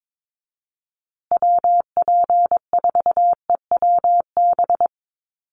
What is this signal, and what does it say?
Morse code, heard as a keyed tone, about 22 wpm, 700 Hz. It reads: WP4EWB
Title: USA Call Signs Repeated 2x 22wpm